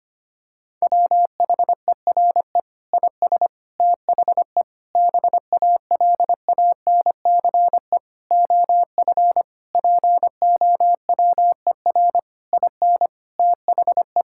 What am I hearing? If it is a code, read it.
Morse code, heard as a keyed tone, about 25 wpm, 700 Hz. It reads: WHERE IS THE BALANCE OF POWER IN THE
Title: Sentences from Top 1000 Words - Review Part 6 - 25wpm